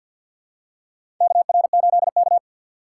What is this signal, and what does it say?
Morse code, heard as a keyed tone, about 53 wpm, 700 Hz. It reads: KR8K
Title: USA Call Signs - 4 Characters - 6x Character Spacing - 50wpm